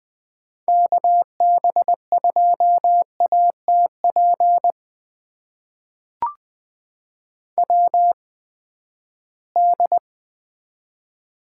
Morse code, heard as a keyed tone, about 20 wpm, spaced as for 14 wpm, 700 Hz. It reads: KB2ATP WD
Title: USA Call Signs - 6 Characters - 8x Character Spacing - 20wpm